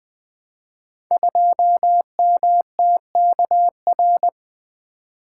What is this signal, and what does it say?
Morse code, heard as a keyed tone, about 20 wpm, 700 Hz. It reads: 2MTKR
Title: Code Groups - 5 Letter:Numbers - 20wpm